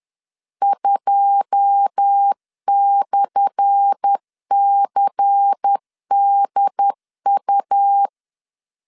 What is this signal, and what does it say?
Morse code, heard as a keyed tone, about 10 wpm, 800 Hz. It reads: 2/CDU
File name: LEZIONI / 10 SIGNS / 10C DOT 10wpm 2940